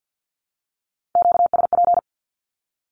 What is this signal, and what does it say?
Morse code, heard as a keyed tone, about 50 wpm, 700 Hz. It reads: QSP
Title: Q-Codes with Spoken Meaning - 50wpm